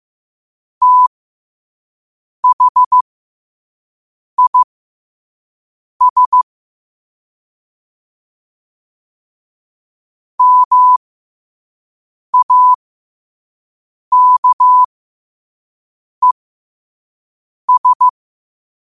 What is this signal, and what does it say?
Morse code, heard as a keyed tone, about 15 wpm, spaced as for 5 wpm, 1000 Hz. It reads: THIS MAKES
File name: qotdfiles / 5wpm